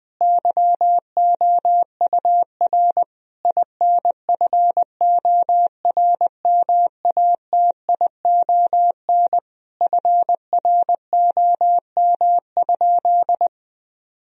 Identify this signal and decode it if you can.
Morse code, heard as a keyed tone, about 21 wpm, 700 Hz. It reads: YOUR INFORMATION FROM?
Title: Sentences from Top 1000 Words - Review Part 3 - 20wpm